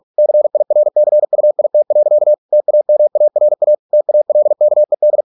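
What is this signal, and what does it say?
Morse code, heard as a keyed tone, about 46 wpm, 600 Hz. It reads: CIRCUIT. TAMARA TALKED
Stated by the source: Morse practice at 45 wpm, the novel Never by Ken Follett in English, from CW med Gnister.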